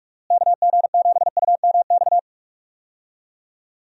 Morse code, heard as a keyed tone, about 46 wpm, 700 Hz. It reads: KG7UMX
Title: USA Call Signs - 6 Characters - Repeated 3x - 45wpm